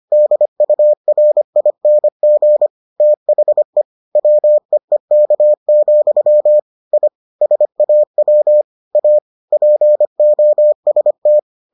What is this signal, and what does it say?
Morse code, heard as a keyed tone, about 25 wpm, 600 Hz. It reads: DURING THE WEEK, I SAW A POST